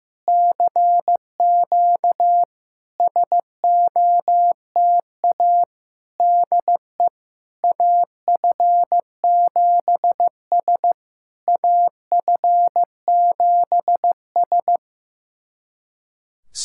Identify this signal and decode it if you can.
Morse code, heard as a keyed tone, about 15 wpm, 700 Hz. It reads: CQ SOTA DE AF7S AF7S
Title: SOTA QSO Practice 15wpm